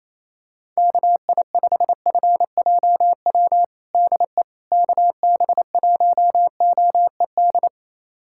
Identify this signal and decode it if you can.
Morse code, heard as a keyed tone, about 28 wpm, 700 Hz. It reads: KI5FJW DE KB1OED
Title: USA Callsign DE Callsign - 6 Characters - 28wpm